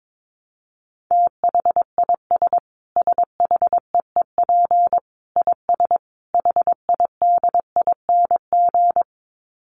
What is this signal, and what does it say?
Morse code, heard as a keyed tone, about 22 wpm, 700 Hz. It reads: THIS SHEEP IS HIDING
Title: Dit Challenge 22wpm